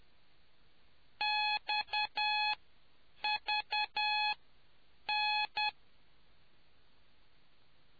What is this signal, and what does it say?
Morse code, heard as a keyed tone, about 10 wpm, spaced as for 7 wpm, 840 Hz. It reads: XVN